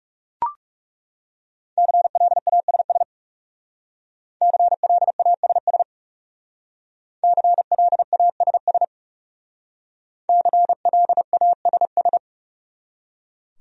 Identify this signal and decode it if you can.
Morse code, heard as a keyed tone, about 38 wpm, 700 Hz. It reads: CLASS CLASS CLASS CLASS